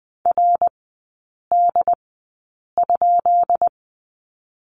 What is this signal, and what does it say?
Morse code, heard as a keyed tone, about 20 wpm, spaced as for 8 wpm, 700 Hz. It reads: RD?
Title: 184 Lesson 30e - Letter b - Three Characters 20wpm